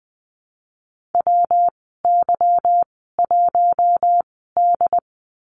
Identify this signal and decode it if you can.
Morse code, heard as a keyed tone, about 19 wpm, spaced as for 14 wpm, 700 Hz. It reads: WY1D
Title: 161 Lesson 26g - Letter g - Call Signs 20wpm